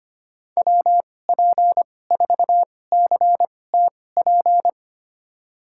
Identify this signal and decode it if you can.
Morse code, heard as a keyed tone, about 25 wpm, spaced as for 18 wpm, 700 Hz. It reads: WP4CTP
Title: USA Call Signs - 6 Characters - 2x Character Spacing - 25wpm